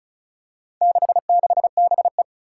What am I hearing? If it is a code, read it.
Morse code, heard as a keyed tone, about 36 wpm, 700 Hz. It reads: 66BE
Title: Four Challenging Letter-Number Combinations 35wpm